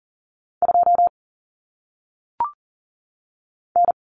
Morse code, heard as a keyed tone, about 39 wpm, 700 Hz. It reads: 2 N
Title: Common Cut Numbers - 1-Digit 40wpm